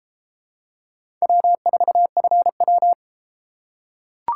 Morse code, heard as a keyed tone, about 30 wpm, 700 Hz. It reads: W4FW